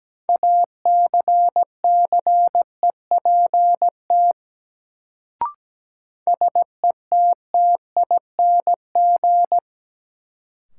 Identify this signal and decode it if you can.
Morse code, heard as a keyed tone, about 17 wpm, 700 Hz. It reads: ACCEPT SETTING
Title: Sentences from Top 1300 Words - New Word Drill Alt - 17wpm